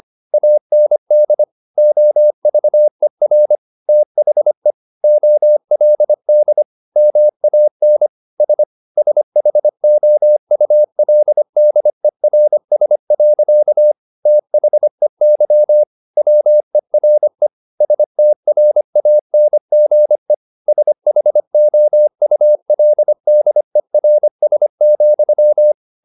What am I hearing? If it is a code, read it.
Morse code, heard as a keyed tone, about 25 wpm, 600 Hz. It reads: AND OVER THE OLD MAN S SHOULDERS. THEY WERE STRANGE SHOULDERS,
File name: the old man and the sea part 1 25wpm